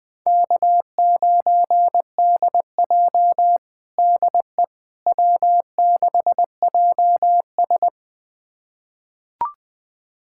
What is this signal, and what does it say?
Morse code, heard as a keyed tone, about 20 wpm, 700 Hz. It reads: K9DJ DE W6JS
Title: USA Callsign DE Callsign - 4 Characters - 20wpm